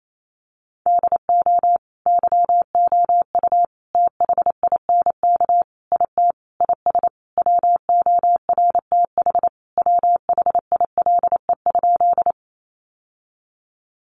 Morse code, heard as a keyed tone, about 28 wpm, 700 Hz. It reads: DO YOU THINK IT IS WORTH WHILE?
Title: Sentences from Top 800 Words 28wpm